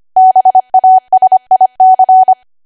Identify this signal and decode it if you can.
Morse code, heard as a keyed tone, about 25 wpm, 750 Hz. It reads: BASIC
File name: cwa / W1AW-Level3 / 25WPM